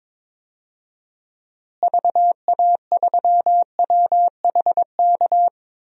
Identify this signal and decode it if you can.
Morse code, heard as a keyed tone, about 22 wpm, 700 Hz. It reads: VA3WHK
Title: DXCC-100 Call Signs - 6 Characters - Repeated 2x - 22wpm